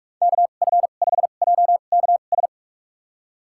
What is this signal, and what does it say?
Morse code, heard as a keyed tone, about 46 wpm, spaced as for 32 wpm, 700 Hz. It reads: KF5JKS
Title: USA Call Signs - 2x Character Spacing - 45wpm